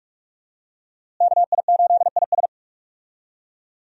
Morse code, heard as a keyed tone, about 41 wpm, 700 Hz. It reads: KI8IS
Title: USA Call Signs - 5 Characters - Repeated 3x - 45wpm